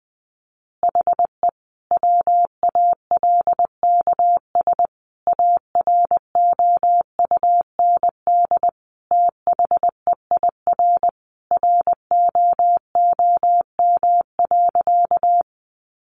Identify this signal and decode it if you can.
Morse code, heard as a keyed tone, about 20 wpm, 700 Hz. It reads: HE WALKS AROUND THEIR ROOM.
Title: Easy Phrases 20wpm